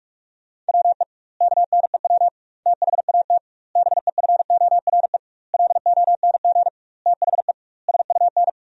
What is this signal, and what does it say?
Morse code, heard as a keyed tone, about 46 wpm, 700 Hz. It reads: WE KNEW THAT BEFORE LONG THE SUN
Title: 2x Word-Spacing - Sentences from Top 400 Words - 45wpm